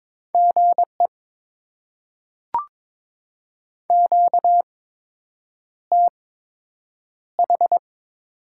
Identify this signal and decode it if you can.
Morse code, heard as a keyed tone, about 21 wpm, spaced as for 8 wpm, 700 Hz. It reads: GE QTH
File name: Top 20 QSO Elements - 8x Character Spacing - 22wpm